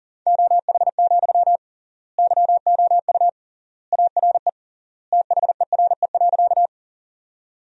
Morse code, heard as a keyed tone, about 40 wpm, 700 Hz. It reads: OH, YOU ARE THERE.
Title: 3x Word-Spacing - Sentences from Top 500 Words - 40wpm